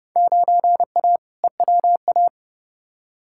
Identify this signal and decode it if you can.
Morse code, heard as a keyed tone, about 30 wpm, 700 Hz. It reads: 9A EWA
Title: Field Day Exchanges Encore - 30wpm